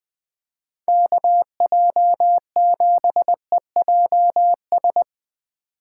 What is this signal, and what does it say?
Morse code, heard as a keyed tone, about 20 wpm, 700 Hz. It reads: KJ7EJS
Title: USA Call Signs - 6 Characters - 2x Character Spacing - 20wpm